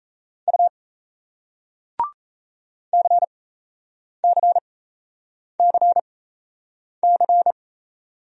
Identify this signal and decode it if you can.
Morse code, heard as a keyed tone, about 38 wpm, spaced as for 8 wpm, 700 Hz. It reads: U CCCC